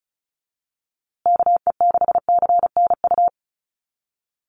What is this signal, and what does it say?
Morse code, heard as a keyed tone, about 34 wpm, 700 Hz. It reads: KE6CNU
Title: USA Call Signs Repeated 2x 35wpm